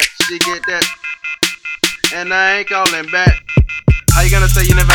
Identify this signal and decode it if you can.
Off-hook warning tone, 1400 + 2060 + 2450 + 2600 Hz, with noise.